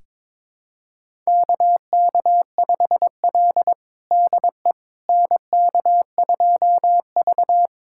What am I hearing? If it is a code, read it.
Morse code, heard as a keyed tone, about 21 wpm, 700 Hz. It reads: KK5L DE NK2V
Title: USA Callsign DE Callsign - 4 Characters - 22wpm